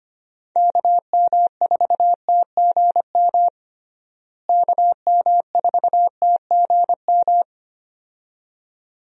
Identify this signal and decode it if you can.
Morse code, heard as a keyed tone, about 25 wpm, 700 Hz. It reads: KM4TGM KM4TGM
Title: USA Call Signs - 6 Characters - Repeated 2x - 25wpm